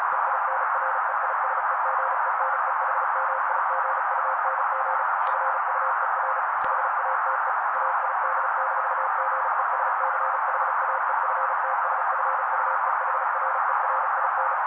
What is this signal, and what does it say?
Morse code, heard as a keyed tone, about 36 wpm, 530 Hz. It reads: EINREISEGENEHMIGUNG FUER HATE AID-GESCHAEFTSFUEHRERIN
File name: daily / 35wpm noise